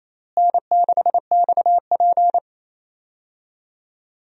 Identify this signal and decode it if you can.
Morse code, heard as a keyed tone, about 28 wpm, 700 Hz. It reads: N6XP
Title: USA Call Signs - 4 Characters - Repeated 2x - 28wpm